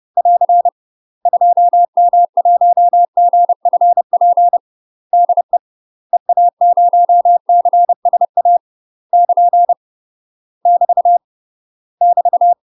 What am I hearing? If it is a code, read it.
Morse code, heard as a keyed tone, about 30 wpm, 700 Hz. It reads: <AR> 2M1GFP DE EA0CSA <KN> <BT> <BT>
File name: UK MASTER QSOs TRAINING / 30WPM / UK MASTER QSOs 0011